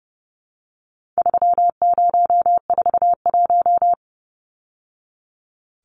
Morse code, heard as a keyed tone, about 30 wpm, 700 Hz. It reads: 3041
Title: Numbers - 4-Digit 30wpm